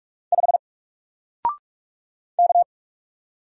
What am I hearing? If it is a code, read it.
Morse code, heard as a keyed tone, about 44 wpm, 700 Hz. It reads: 5 K
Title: Single Letter-Number Alt 45wpm